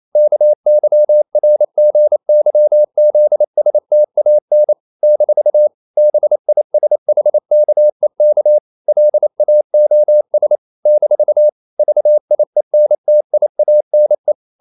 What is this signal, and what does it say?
Morse code, heard as a keyed tone, about 28 wpm, 600 Hz. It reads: KYRGYZSTAN - BISHKEK LAOS - VIENTIANE
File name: CW / Practice / NationalCapitals28